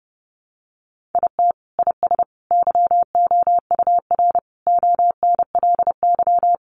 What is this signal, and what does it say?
Morse code, heard as a keyed tone, about 30 wpm, 700 Hz. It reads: IT IS YOUR ONLY
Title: Sentences from Top 800 Words 30wpm